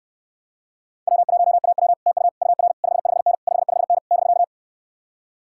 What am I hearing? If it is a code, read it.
Morse code, heard as a keyed tone, about 69 wpm, 700 Hz. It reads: W1AW TU UR 55N 55N <BK>